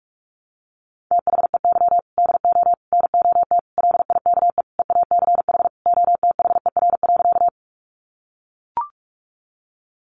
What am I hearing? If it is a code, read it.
Morse code, heard as a keyed tone, about 44 wpm, 700 Hz. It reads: THEY DO NOT LIKE EACH OTHER.
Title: Easy Phrases 45wpm